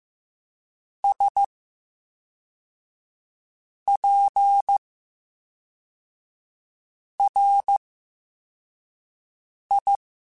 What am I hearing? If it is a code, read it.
Morse code, heard as a keyed tone, about 15 wpm, spaced as for 4 wpm, 800 Hz. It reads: SPRI
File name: Special INGs / Special INGs-mp3s / VerySlowToFaster Ing @10